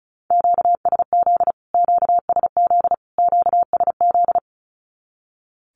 Morse code, heard as a keyed tone, about 34 wpm, 700 Hz. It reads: QSZ QSZ QSZ
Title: Q-Codes Repeated 3x 35wpm